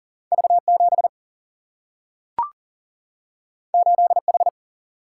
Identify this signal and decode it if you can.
Morse code, heard as a keyed tone, about 40 wpm, 700 Hz. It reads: V7 8H
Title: Single-Letter Single-Number 40wpm